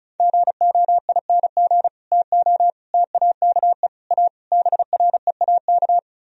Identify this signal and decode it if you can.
Morse code, heard as a keyed tone, about 38 wpm, 700 Hz. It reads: GOING TO TAKE A BREAK